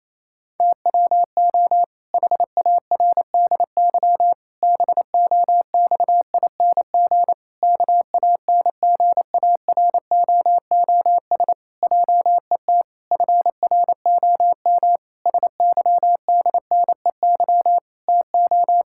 Morse code, heard as a keyed tone, about 28 wpm, 700 Hz. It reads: TWO HARDY BOXING KANGAROOS JET FROM SYDNEY TO
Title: Pangram Challenge v2 - 28wpm